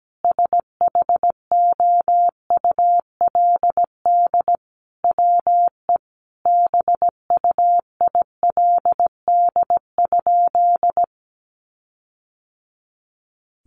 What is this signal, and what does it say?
Morse code, heard as a keyed tone, about 17 wpm, 700 Hz. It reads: SHOULD WE BUILD?
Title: Sentences from Top 1300 Words 17wpm